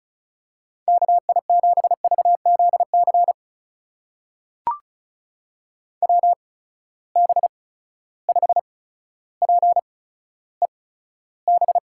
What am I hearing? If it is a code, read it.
Morse code, heard as a keyed tone, about 36 wpm, spaced as for 15 wpm, 700 Hz. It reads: KI7VZC WB5PEB
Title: USA Call Signs - 6 Characters - 8x Character Spacing - 35wpm